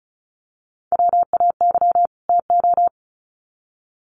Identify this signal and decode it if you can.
Morse code, heard as a keyed tone, about 34 wpm, 700 Hz. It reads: WAY TO
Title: Sets of 2 Words 35wpm